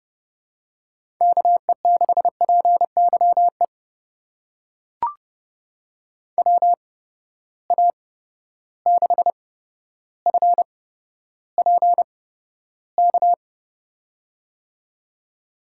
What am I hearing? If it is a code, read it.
Morse code, heard as a keyed tone, about 30 wpm, spaced as for 13 wpm, 700 Hz. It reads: KE6PYE WA6FPK
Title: USA Call Signs - 6 Characters - 8x Character Spacing - 30wpm